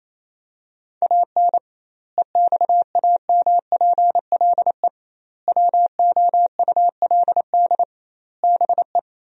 Morse code, heard as a keyed tone, about 28 wpm, 700 Hz. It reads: AN EXAMPLE WOULD BE
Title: 2x Word-Spacing - Sentences from Top 400 Words - 28wpm